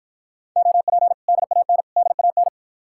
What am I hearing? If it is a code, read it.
Morse code, heard as a keyed tone, about 53 wpm, 700 Hz. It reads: OP DAN DAN